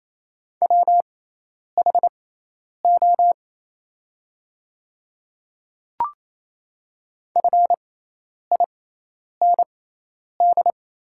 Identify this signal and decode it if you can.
Morse code, heard as a keyed tone, about 28 wpm, spaced as for 10 wpm, 700 Hz. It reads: WHO FIND
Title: 6x Character Spacing - Top 300 Words - 28wpm